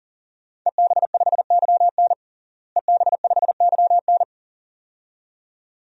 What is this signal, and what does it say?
Morse code, heard as a keyed tone, about 37 wpm, 700 Hz. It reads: EB5YN EB5YN
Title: DXCC-100 Call Signs - 5 Characters - Repeated 2x - 40wpm